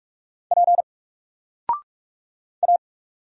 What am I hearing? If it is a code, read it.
Morse code, heard as a keyed tone, about 44 wpm, 700 Hz. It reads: P A
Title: Single Character (Letter:Number:Punctuation:Prosign) 45wpm